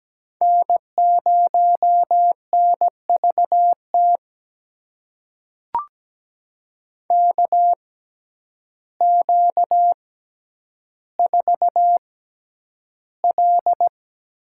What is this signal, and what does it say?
Morse code, heard as a keyed tone, about 17 wpm, spaced as for 9 wpm, 700 Hz. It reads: N0NVT KQ4L
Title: USA Call Signs - 5 Characters - 6x Character Spacing - 17wpm